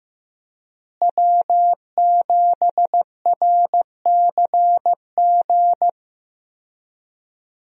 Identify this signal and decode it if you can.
Morse code, heard as a keyed tone, about 15 wpm, 700 Hz. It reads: W7RCG
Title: USA Call Signs - 5 Characters - Repeated 2x - 15wpm